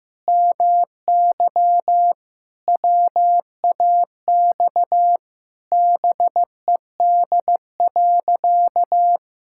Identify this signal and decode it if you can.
Morse code, heard as a keyed tone, about 15 wpm, 700 Hz. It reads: MY WAX BED.
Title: Pangram Challenge v2 - 15wpm